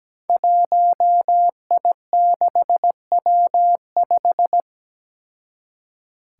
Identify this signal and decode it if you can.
Morse code, heard as a keyed tone, about 17 wpm, 700 Hz. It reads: 1I6W5
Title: Five Challenging Letter-Number Combinations 17wpm